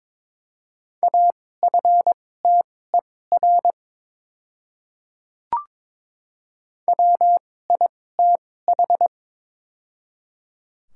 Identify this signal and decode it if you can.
Morse code, heard as a keyed tone, about 22 wpm, spaced as for 16 wpm, 700 Hz. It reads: AFTER WITH
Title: 2x Character Spacing - Top 200 Words - 22wpm